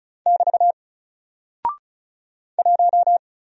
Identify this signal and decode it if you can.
Morse code, heard as a keyed tone, about 36 wpm, 700 Hz. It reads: <BT> 1
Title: Single Character (Letter:Number:Punctuation:Prosign) Alt 35wpm